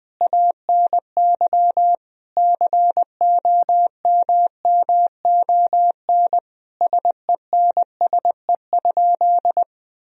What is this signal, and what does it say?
Morse code, heard as a keyed tone, about 21 wpm, 700 Hz. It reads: ANY COMMON SENSE?
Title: Sentences from Top 1000 Words - Review Part 4 - 20wpm